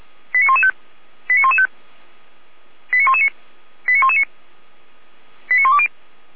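Five-tone selective call (ZVEI), heard as quick runs of five tones, tones of about 70 ms. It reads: 89115 89115 89119 89119 89120
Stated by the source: ZVEI five-tone call 89042 received on an SDR.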